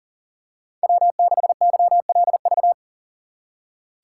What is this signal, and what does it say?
Morse code, heard as a keyed tone, about 40 wpm, 700 Hz. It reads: W6YLV
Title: USA Call Signs - 2x Character Spacing - 40wpm